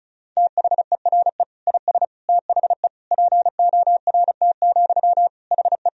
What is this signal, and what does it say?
Morse code, heard as a keyed tone, about 36 wpm, 700 Hz. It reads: THERE IS THE PORT, HE
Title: Sentences from Top 1000 Words - Review Part 1 - 35wpm